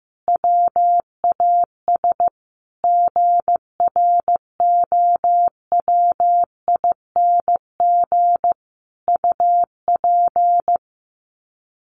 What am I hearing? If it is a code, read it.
Morse code, heard as a keyed tone, about 15 wpm, 700 Hz. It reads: WAS GROWING UP
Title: Sets of 5 Words Encore 15wpm